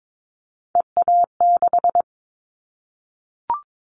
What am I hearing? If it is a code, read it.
Morse code, heard as a keyed tone, about 22 wpm, 700 Hz. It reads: EA6
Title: Callsign Prefixes for Top 50 DXCC - 22wpm